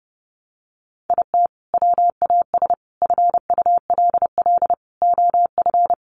Morse code, heard as a keyed tone, about 30 wpm, 700 Hz. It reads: IT WAS FULL OF
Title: Sentences from Top 500 Words 30wpm